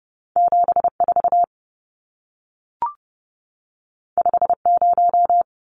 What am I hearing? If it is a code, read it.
Morse code, heard as a keyed tone, about 30 wpm, 700 Hz. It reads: 74 50
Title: Numbers - 2-Digit 30wpm